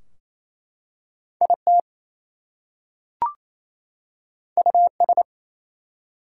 Morse code, heard as a keyed tone, about 27 wpm, 700 Hz. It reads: IT US
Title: Two Letter Words - 28wpm